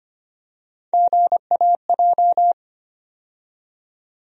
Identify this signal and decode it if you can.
Morse code, heard as a keyed tone, about 25 wpm, 700 Hz. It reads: GAJ
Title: USA Call Sign Postfixes - 3 Letters 25wpm